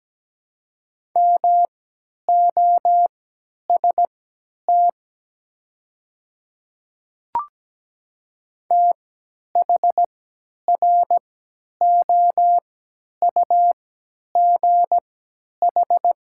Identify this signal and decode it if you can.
Morse code, heard as a keyed tone, about 17 wpm, spaced as for 10 wpm, 700 Hz. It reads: MOST THROUGH
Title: 3x Character Spacing - Top 200 Words - 17wpm